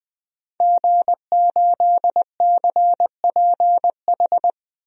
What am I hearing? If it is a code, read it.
Morse code, heard as a keyed tone, about 20 wpm, 700 Hz. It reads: G8CPH
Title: DXCC-100 Call Signs - 5 Characters - 20wpm